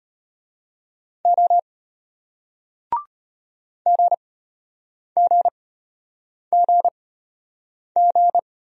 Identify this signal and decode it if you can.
Morse code, heard as a keyed tone, about 33 wpm, spaced as for 8 wpm, 690 Hz. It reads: O GGGG